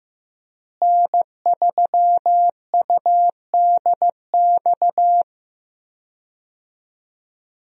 Morse code, heard as a keyed tone, about 15 wpm, 700 Hz. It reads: N3UDX
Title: USA Call Signs - 5 Characters - Repeated 3x - 15wpm